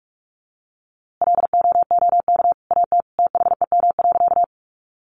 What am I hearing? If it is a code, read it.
Morse code, heard as a keyed tone, about 44 wpm, 700 Hz. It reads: LOOK AT THEM.
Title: A Sound Of Thunder 45wpm